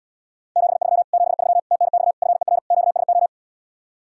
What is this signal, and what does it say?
Morse code, heard as a keyed tone, about 77 wpm, 700 Hz. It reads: 73 73 TNX FER QSO